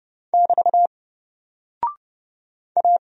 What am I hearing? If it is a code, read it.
Morse code, heard as a keyed tone, about 30 wpm, 700 Hz. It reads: <BT> A
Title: Single Character (Letter:Number:Punctuation:Prosign) 30wpm